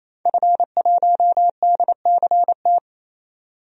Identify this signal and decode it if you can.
Morse code, heard as a keyed tone, about 28 wpm, 700 Hz. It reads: F1DCT
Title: DXCC-100 Call Signs - 5 Characters - 28wpm